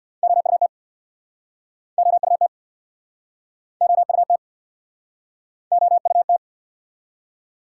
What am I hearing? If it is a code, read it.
Morse code, heard as a keyed tone, about 69 wpm, 700 Hz. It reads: OUT OUT OUT OUT